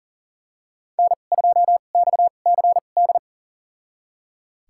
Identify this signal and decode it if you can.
Morse code, heard as a keyed tone, about 40 wpm, spaced as for 29 wpm, 700 Hz. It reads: N2XCD
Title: USA Call Signs - 2x Character Spacing - 40wpm